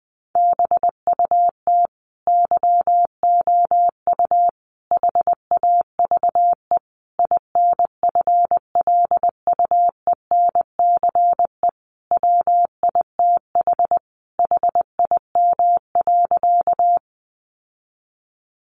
Morse code, heard as a keyed tone, about 20 wpm, 700 Hz. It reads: BUT YOU HAVE INFLUENCE WITH HIM.
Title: Sentences from Top 1100 Words 20wpm